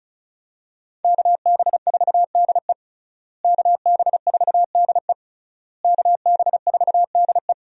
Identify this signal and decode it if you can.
Morse code, heard as a keyed tone, about 36 wpm, 700 Hz. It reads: KB4DE KB4DE KB4DE
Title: USA Call Signs - 5 Characters - Repeated 3x - 35wpm